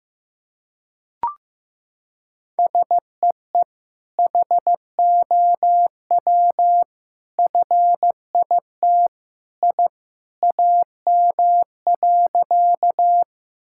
Morse code, heard as a keyed tone, about 15 wpm, 700 Hz. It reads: SEE HOW FIT I AM.
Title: Sentences from Top 1000 Words - Review Part 5 - 15wpm